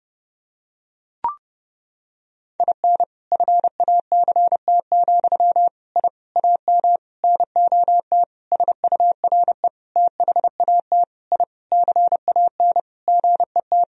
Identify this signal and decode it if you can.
Morse code, heard as a keyed tone, about 30 wpm, 700 Hz. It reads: IN FACT, I AM NOT SURE THAT I CAN GET
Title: Sentences from Top 1000 Words - Review Part 3 - 30wpm